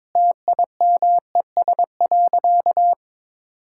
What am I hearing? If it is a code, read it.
Morse code, heard as a keyed tone, about 22 wpm, 700 Hz. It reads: TIMES.
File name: practice / 22wpm / Sentences with 4-Word Phrases - 22wpm